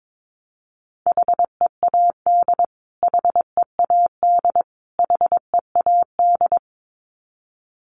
Morse code, heard as a keyed tone, about 22 wpm, 700 Hz. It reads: HEAD HEAD HEAD
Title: Top 500 Words Repeated 22wpm